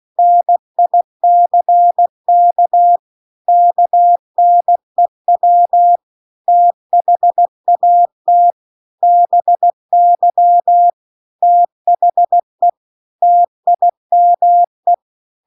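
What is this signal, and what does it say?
Morse code, heard as a keyed tone, about 16 wpm, 700 Hz. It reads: NICK KNEW THAT BY THE TIME
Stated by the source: Morse practice text at 16 WPM.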